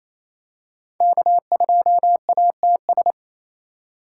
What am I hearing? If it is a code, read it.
Morse code, heard as a keyed tone, about 27 wpm, 700 Hz. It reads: K2ATS
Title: USA Call Signs - 3x Character Spacing - 28wpm